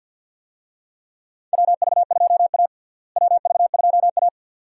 Morse code, heard as a keyed tone, about 52 wpm, 700 Hz. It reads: WV2A WV2A
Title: USA Call Signs - 4 Characters - Repeated 2x - 50wpm